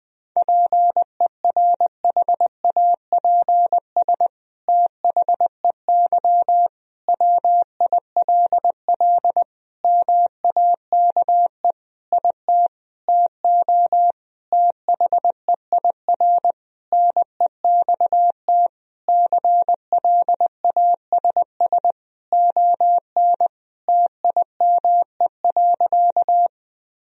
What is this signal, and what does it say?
Morse code, heard as a keyed tone, about 20 wpm, 700 Hz. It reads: PERHAPS THEY WILL MAKE IT TO THEIR NEXT CLASS ON TIME.
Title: Sentences from Top 1000 Words - Review Part 2 - 20wpm